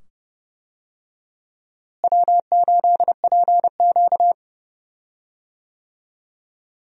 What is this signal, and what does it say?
Morse code, heard as a keyed tone, about 30 wpm, 700 Hz. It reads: W8PQ
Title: USA Call Signs - 4 Characters - Repeated 2x - 30wpm